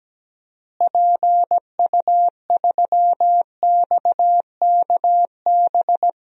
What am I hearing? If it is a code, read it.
Morse code, heard as a keyed tone, about 17 wpm, 700 Hz. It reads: PU3XKB
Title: DXCC-100 Call Signs - 6 Characters - Repeated 2x - 17wpm